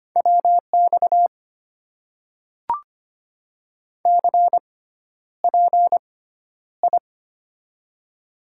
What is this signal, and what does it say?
Morse code, heard as a keyed tone, about 25 wpm, spaced as for 11 wpm, 700 Hz. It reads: WX CPI
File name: Top 90 QSO Elements - 6x Character Spacing - 25wpm